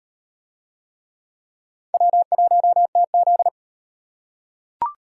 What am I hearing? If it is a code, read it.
Morse code, heard as a keyed tone, about 38 wpm, 700 Hz. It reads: W1TZ